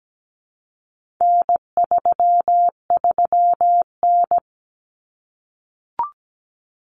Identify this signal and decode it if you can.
Morse code, heard as a keyed tone, about 17 wpm, 700 Hz. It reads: N33N
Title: Common Cut Numbers - 4-Digit 17wpm